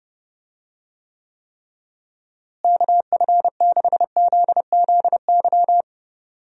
Morse code, heard as a keyed tone, about 30 wpm, 700 Hz. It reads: KF6ZZY
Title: USA Call Signs - 6 Characters - Repeated 3x - 30wpm